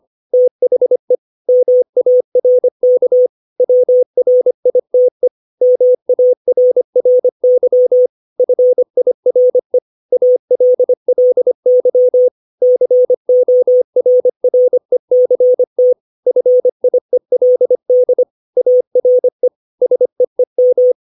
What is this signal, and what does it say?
Morse code, heard as a keyed tone, about 25 wpm, 500 Hz. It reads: THE MARK WRITE MARRY FIRE ALLY CORRECT FIELD ARE SEEM
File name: 632 words / 632 Words 25WPM